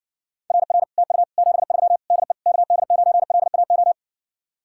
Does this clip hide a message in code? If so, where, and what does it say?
Yes: Morse code from 0.5 to 3.9 s, heard as a keyed tone, about 60 wpm, 700 Hz. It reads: RR TU 73 DE KD9LAO